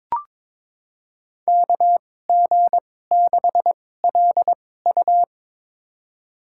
Morse code, heard as a keyed tone, about 22 wpm, spaced as for 16 wpm, 700 Hz. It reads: KG6LU
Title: USA Call Signs - 5 Characters - 2x Character Spacing - 22wpm